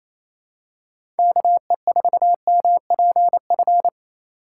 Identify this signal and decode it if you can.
Morse code, heard as a keyed tone, about 28 wpm, 700 Hz. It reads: KE4MPF
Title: USA Call Signs - 6 Characters - 2x Character Spacing - 28wpm